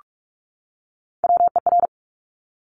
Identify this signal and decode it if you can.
Morse code, heard as a keyed tone, about 46 wpm, 700 Hz. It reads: WER
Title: Top Three Letter Combinations in Words 45wpm